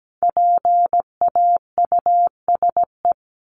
Morse code, heard as a keyed tone, about 17 wpm, 700 Hz. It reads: PAUSE
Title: Top 1501-1750 Words 17wpm